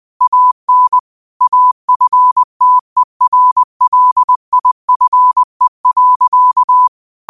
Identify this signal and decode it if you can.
Morse code, heard as a keyed tone, about 20 wpm, 1000 Hz. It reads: AN AFTERLIFE.
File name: qotdfiles / 20wpm